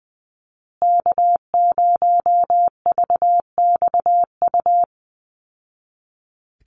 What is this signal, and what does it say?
Morse code, heard as a keyed tone, about 20 wpm, 700 Hz. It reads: K0VXU
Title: Call Signs Repeated 3x - 20wpm